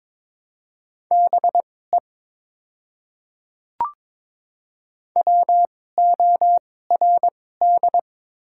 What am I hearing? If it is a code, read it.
Morse code, heard as a keyed tone, about 21 wpm, spaced as for 16 wpm, 700 Hz. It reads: BE WORD
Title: 2x Character Spacing - Top 100 Words - 22wpm